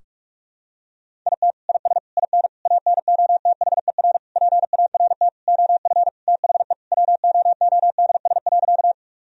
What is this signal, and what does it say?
Morse code, heard as a keyed tone, about 41 wpm, 700 Hz. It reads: IT IS IN ANOTHER PART OF THE WOODS.
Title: Sentences from Top 1000 Words - Review Part 2 - 45wpm